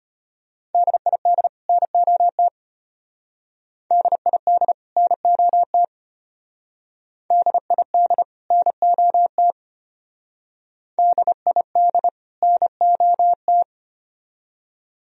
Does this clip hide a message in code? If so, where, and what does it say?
Yes: Morse code from 0.7 to 13.6 s, heard as a keyed tone, about 31 wpm, 700 Hz. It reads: DID NOT DID NOT DID NOT DID NOT